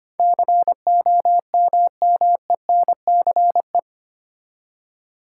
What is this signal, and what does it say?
Morse code, heard as a keyed tone, about 25 wpm, 700 Hz. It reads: COMMENCE
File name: Eight Letter Words - 25wpm